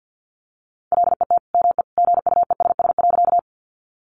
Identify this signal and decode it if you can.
Morse code, heard as a keyed tone, about 50 wpm, 700 Hz. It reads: LET ME GUESS.
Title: Sentences from Top 1500 Words 50wpm